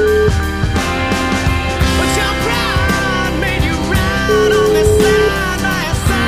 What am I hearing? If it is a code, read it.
Ringback tone (Europe, 425 Hz), with noise.